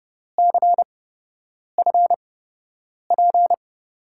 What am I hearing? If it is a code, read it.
Morse code, heard as a keyed tone, about 30 wpm, spaced as for 8 wpm, 700 Hz. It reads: CFP
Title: USA Call Signs - 8x Character Spacing - 30wpm